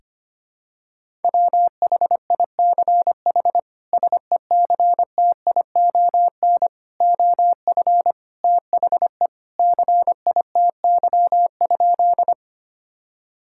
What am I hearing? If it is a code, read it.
Morse code, heard as a keyed tone, about 25 wpm, 700 Hz. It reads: WHICH SECTION OF THE CITY?
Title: Sentences from Top 1000 Words - Review Part 5 - 25wpm